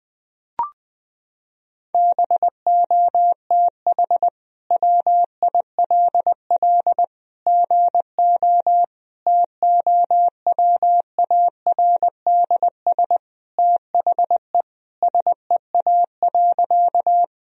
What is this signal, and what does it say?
Morse code, heard as a keyed tone, about 20 wpm, 700 Hz. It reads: BOTH WILL GO TOWARDS THE SEA.
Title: Sentences from Top 1000 Words - Review Part 2 - 20wpm